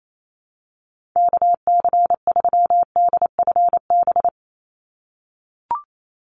Morse code, heard as a keyed tone, about 28 wpm, 700 Hz. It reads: KC3DFB
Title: USA Call Signs 28wpm